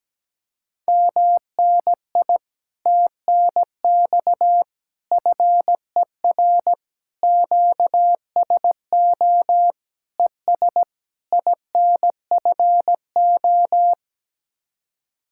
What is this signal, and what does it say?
Morse code, heard as a keyed tone, about 17 wpm, 700 Hz. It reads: MNI TNX FER QSO ES INFO
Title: Speed-Racing Common QSO Phrases Encore - 17wpm